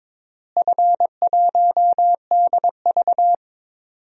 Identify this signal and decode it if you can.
Morse code, heard as a keyed tone, about 22 wpm, 700 Hz. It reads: F1DV
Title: DXCC-100 Call Signs - 4 Characters - Repeated 2x - 22wpm